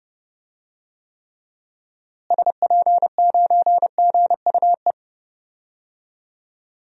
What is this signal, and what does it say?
Morse code, heard as a keyed tone, about 30 wpm, 700 Hz. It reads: SP9GUE
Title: DXCC-100 Call Signs - 6 Characters - Repeated 3x - 30wpm